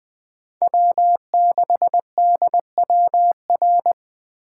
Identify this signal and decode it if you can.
Morse code, heard as a keyed tone, about 20 wpm, 700 Hz. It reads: W6DWR
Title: USA Call Signs - 5 Characters - 4x Character Spacing - 20wpm